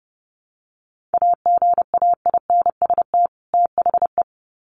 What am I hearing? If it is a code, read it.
Morse code, heard as a keyed tone, about 30 wpm, 700 Hz. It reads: AGAINST THE
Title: Sets of 2 Words 30wpm